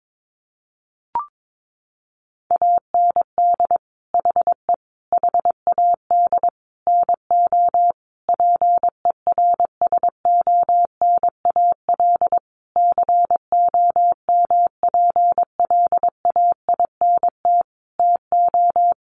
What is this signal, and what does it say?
Morse code, heard as a keyed tone, about 22 wpm, 700 Hz. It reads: AND HE HAD NO PERSONAL COMPLAINT TO
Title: Sentences from Top 1400 Words 22wpm